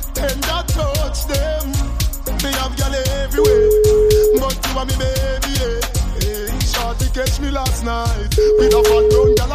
Ringback tone (Europe, 425 Hz), with noise.